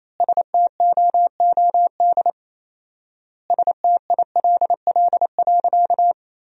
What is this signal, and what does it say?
Morse code, heard as a keyed tone, about 28 wpm, 700 Hz. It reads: STOOD STILL.
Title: 4x Word-Spacing - Sentences from Top 500 Words - 28wpm